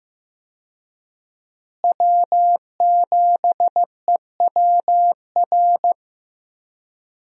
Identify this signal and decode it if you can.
Morse code, heard as a keyed tone, about 15 wpm, 700 Hz. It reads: W7EWR
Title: USA Call Signs - 5 Characters - Repeated 2x - 15wpm